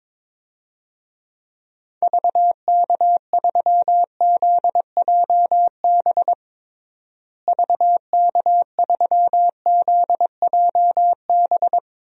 Morse code, heard as a keyed tone, about 22 wpm, 700 Hz. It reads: VK3ZJB VK3ZJB
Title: DXCC-100 Call Signs - 6 Characters - Repeated 2x - 22wpm